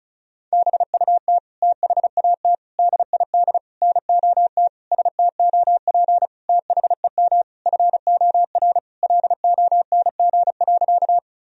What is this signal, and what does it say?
Morse code, heard as a keyed tone, about 34 wpm, 700 Hz. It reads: BUT THAT DID NOT STOP THEM FOR LONG.
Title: Sentences from Top 1000 Words - Review Part 1 - 35wpm